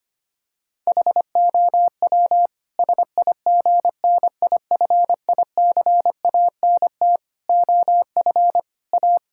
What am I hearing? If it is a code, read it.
Morse code, heard as a keyed tone, about 25 wpm, 700 Hz. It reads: HOW SIGNIFICANT OF A
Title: Sentences from Top 1000 Words - Review Part 6 - 25wpm